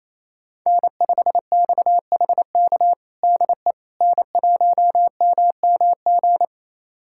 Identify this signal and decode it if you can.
Morse code, heard as a keyed tone, about 28 wpm, 700 Hz. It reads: N5XHK DE N1MMG
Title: USA Callsign DE Callsign - 5 Characters - 28wpm